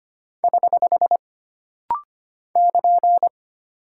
Morse code, heard as a keyed tone, about 25 wpm, 700 Hz. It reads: <HH> <KN>
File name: Single Punctuation-Prosign - Rapid-Fire - 25wpm